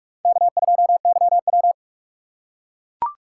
Morse code, heard as a keyed tone, about 43 wpm, 700 Hz. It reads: K2YW
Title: USA Call Signs - 4 Characters - 3x Character Spacing - 45wpm